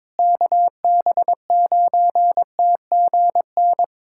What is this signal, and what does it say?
Morse code, heard as a keyed tone, about 22 wpm, 700 Hz. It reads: KB9TGN
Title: USA Callsign DE Callsign - 6 Characters - 22wpm